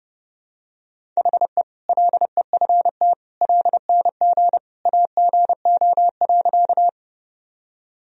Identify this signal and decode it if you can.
Morse code, heard as a keyed tone, about 32 wpm, 700 Hz. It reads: HE LEFT LNG AGO.
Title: Sentences from Top 500 Words and Abbreviations 30wpm